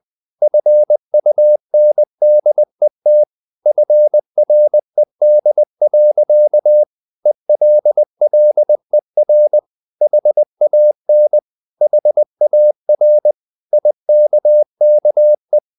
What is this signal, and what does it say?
Morse code, heard as a keyed tone, about 20 wpm, 600 Hz. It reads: FUNDET FRED. ELLER HAN HAR IKKE
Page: Ferie 20 wpm